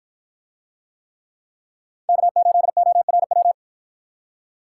Morse code, heard as a keyed tone, about 53 wpm, 700 Hz. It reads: K8ORW